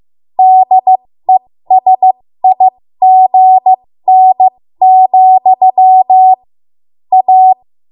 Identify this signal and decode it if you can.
Morse code, heard as a keyed tone, about 15 wpm, spaced as for 13 wpm, 750 Hz. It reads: DESIGN, A
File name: Morse / 13WPM